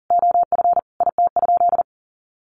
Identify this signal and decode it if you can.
Morse code, heard as a keyed tone, about 40 wpm, 700 Hz. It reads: OF IT?
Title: Sentences from Top 1100 Words 40wpm